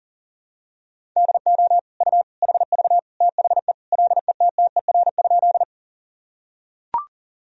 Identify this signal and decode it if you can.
Morse code, heard as a keyed tone, about 37 wpm, 700 Hz. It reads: DO U HV THE LETTER?
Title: Sentences from Top 300 Words and Abbreviations 40wpm